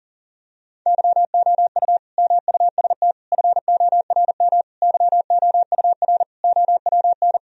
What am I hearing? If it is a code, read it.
Morse code, heard as a keyed tone, about 40 wpm, 700 Hz. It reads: YOU MUST FORM YOUR OWN
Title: Sentences from Top 1000 Words - Review Part 1 - 40wpm